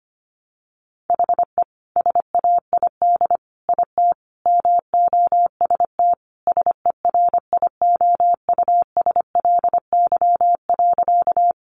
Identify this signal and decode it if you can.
Morse code, heard as a keyed tone, about 25 wpm, 700 Hz. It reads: HE SAID IT MOST SERIOUSLY.
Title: Sentences from Top 1200 Words 25wpm